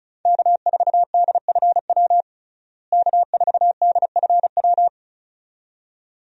Morse code, heard as a keyed tone, about 36 wpm, 700 Hz. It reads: K4DFW K4DFW
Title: USA Call Signs - 5 Characters - Repeated 2x - 35wpm